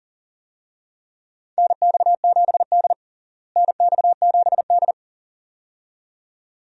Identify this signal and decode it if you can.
Morse code, heard as a keyed tone, about 40 wpm, 700 Hz. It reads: NX7D NX7D
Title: USA Call Signs - 4 Characters - Repeated 2x - 40wpm